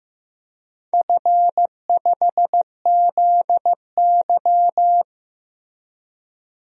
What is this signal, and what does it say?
Morse code, heard as a keyed tone, about 16 wpm, 700 Hz. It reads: F5ZY
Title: DXCC-100 Call Signs - 4 Characters - Repeated 2x - 15wpm